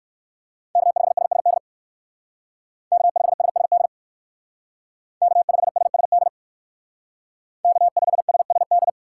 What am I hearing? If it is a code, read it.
Morse code, heard as a keyed tone, about 59 wpm, 700 Hz. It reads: K5SSD K5SSD K5SSD K5SSD